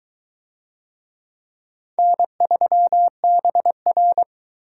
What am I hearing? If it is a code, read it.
Morse code, heard as a keyed tone, about 22 wpm, 700 Hz. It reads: N3BR